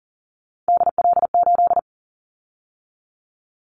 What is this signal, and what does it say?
Morse code, heard as a keyed tone, about 41 wpm, 700 Hz. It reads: DL8
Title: Callsign Prefixes for Top 50 DXCC - 40wpm